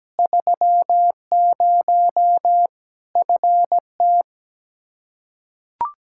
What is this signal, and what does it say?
Morse code, heard as a keyed tone, about 17 wpm, 700 Hz. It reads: 30 FT
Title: Practice QSO - Exchange 2 - Info - 17wpm